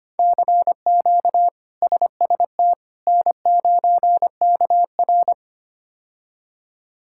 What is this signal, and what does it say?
Morse code, heard as a keyed tone, about 25 wpm, 700 Hz. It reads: CQ SST N9KR
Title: K1USN SST QSO Practice - 25wpm